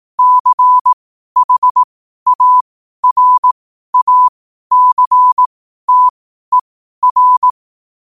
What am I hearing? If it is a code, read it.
Morse code, heard as a keyed tone, about 18 wpm, spaced as for 12 wpm, 1000 Hz. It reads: CHARACTER
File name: qotdfiles / 12wpm